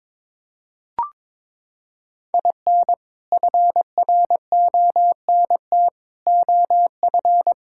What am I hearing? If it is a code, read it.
Morse code, heard as a keyed tone, about 22 wpm, 700 Hz. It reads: IN FRONT OF
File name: practice / 22wpm / Sentences with 4-Word Phrases - 22wpm